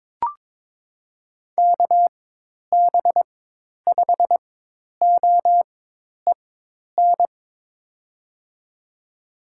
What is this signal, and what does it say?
Morse code, heard as a keyed tone, about 22 wpm, spaced as for 10 wpm, 700 Hz. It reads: KB5OEN
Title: USA Call Signs - 6 Characters - 4x Character Spacing - 22wpm